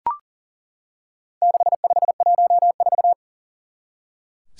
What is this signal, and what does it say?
Morse code, heard as a keyed tone, about 40 wpm, 700 Hz. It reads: 6514
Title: Four Challenging Letter-Number Combinations 40wpm